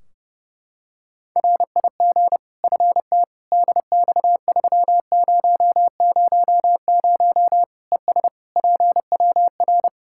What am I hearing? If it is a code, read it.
Morse code, heard as a keyed tone, about 32 wpm, 700 Hz. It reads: RIG FT DX3000 ES PWR
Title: Practice QSO - Exchange 2 - Info - 30wpm